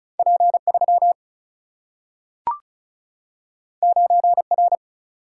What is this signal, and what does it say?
Morse code, heard as a keyed tone, about 36 wpm, 700 Hz. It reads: P3 9R
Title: Single-Letter Single-Number 35wpm